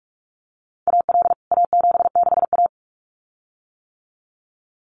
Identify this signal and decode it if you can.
Morse code, heard as a keyed tone, about 44 wpm, 700 Hz. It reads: AL A76A
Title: CWT Exchange 45wpm